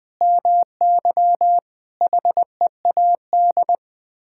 Morse code, heard as a keyed tone, about 20 wpm, 700 Hz. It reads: MY HEAD
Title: Sets of 6 Words - 20wpm